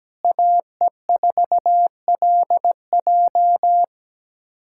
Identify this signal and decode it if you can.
Morse code, heard as a keyed tone, about 17 wpm, 700 Hz. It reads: AE4LJ
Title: USA Callsign DE Callsign - 5 Characters - 17wpm